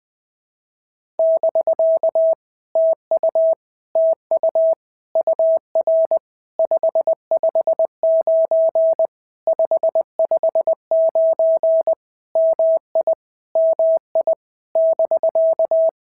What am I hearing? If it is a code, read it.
Morse code, heard as a keyed tone, about 20 wpm, 650 Hz. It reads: <BK> TU TU UR 559 559 MI MI <BK>
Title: POTA QSO Practice - 20wpm